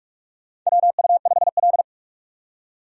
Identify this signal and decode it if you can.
Morse code, heard as a keyed tone, about 46 wpm, 700 Hz. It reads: WU5L
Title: USA Call Signs - 4 Characters - 45wpm